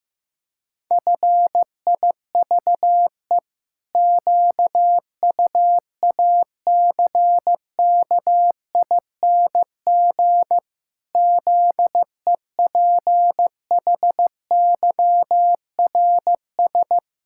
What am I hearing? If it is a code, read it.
Morse code, heard as a keyed tone, about 15 wpm, 700 Hz. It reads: FIVE QUACKING ZEPHYRS
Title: Pangram Challenge v2 - 15wpm